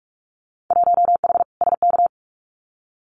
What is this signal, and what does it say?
Morse code, heard as a keyed tone, about 44 wpm, 700 Hz. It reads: 1H SK
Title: Winter Field Day Exchanges 45wpm